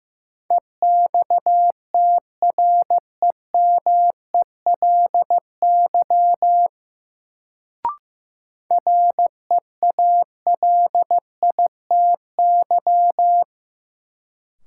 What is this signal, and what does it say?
Morse code, heard as a keyed tone, about 15 wpm, 700 Hz. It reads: EXTREMELY REALITY
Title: Sentences from Top 1200 Words - New Word Drill Alt - 15wpm